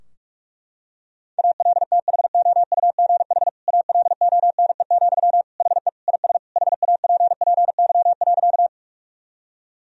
Morse code, heard as a keyed tone, about 41 wpm, 700 Hz. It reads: ALTHOUGH ALONE, HE IS HAPPY.
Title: Sentences from Top 1000 Words - Review Part 3 - 45wpm